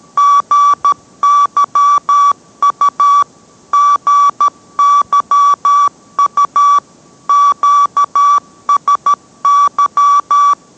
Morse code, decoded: GYU GYU QSY

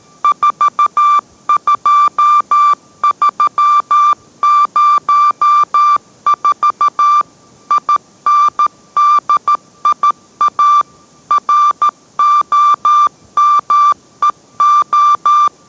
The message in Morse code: 42304 INDIA ROMEO

